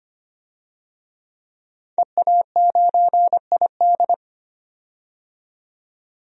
Morse code, heard as a keyed tone, about 25 wpm, 700 Hz. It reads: EA9ID
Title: DXCC-100 Call Signs - 5 Characters - 25wpm